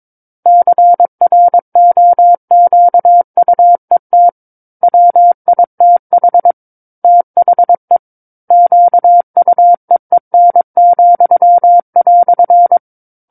Morse code, heard as a keyed tone, about 22 wpm, 700 Hz. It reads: CROQUET WITH THE QUEEN,"
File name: 06-Chapter-06 22 WPM